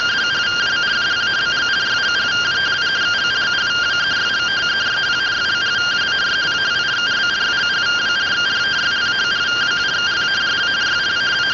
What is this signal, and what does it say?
Two-tone teleprinter signal, about 170 Hz shift.